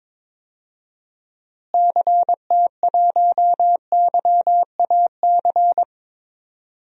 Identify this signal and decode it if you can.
Morse code, heard as a keyed tone, about 22 wpm, 700 Hz. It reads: CT1YAC
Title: DXCC-100 Call Signs - 6 Characters - Repeated 2x - 22wpm